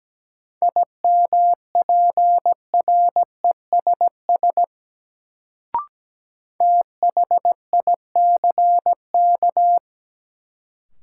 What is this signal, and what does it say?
Morse code, heard as a keyed tone, about 17 wpm, 700 Hz. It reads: IMPRESS THICK
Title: Sentences from Top 1600 Words - New Word Drill - 17wpm